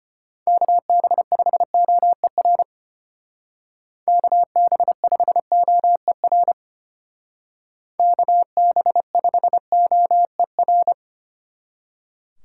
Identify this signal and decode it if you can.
Morse code, heard as a keyed tone, about 30 wpm, 700 Hz. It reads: KB5OER KB5OER KB5OER
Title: Speed Racing - USA 2x3 Call Signs 25wpm